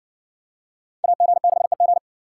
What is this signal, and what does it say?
Morse code, heard as a keyed tone, about 58 wpm, 700 Hz. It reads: AG6EG